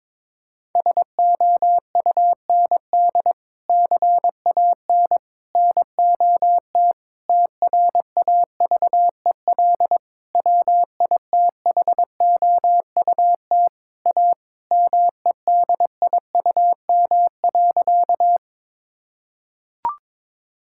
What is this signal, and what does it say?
Morse code, heard as a keyed tone, about 22 wpm, 700 Hz. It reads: SOUND CAN NOT TRAVEL WITHOUT A MEDIUM.
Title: Sentences from Top 1000 Words - Review Part 6 - 22wpm